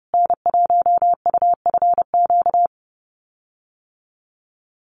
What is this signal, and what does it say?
Morse code, heard as a keyed tone, about 30 wpm, 700 Hz. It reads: N1UFQ
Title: USA Call Signs Repeated 3x 30wpm